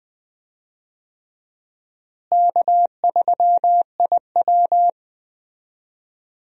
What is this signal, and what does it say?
Morse code, heard as a keyed tone, about 20 wpm, 700 Hz. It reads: K3IW
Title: USA Call Signs - 4 Characters - Repeated 3x - 20wpm